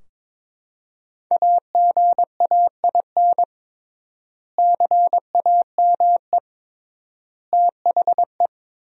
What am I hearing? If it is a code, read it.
Morse code, heard as a keyed tone, about 21 wpm, 700 Hz. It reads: AGAIN CAME THE
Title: 3x Word-Spacing - Sentences from Top 500 Words - 22wpm